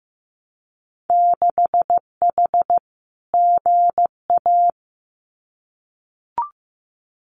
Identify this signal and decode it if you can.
Morse code, heard as a keyed tone, about 15 wpm, 700 Hz. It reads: 6H GA
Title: Winter Field Day Exchanges Encore 15wpm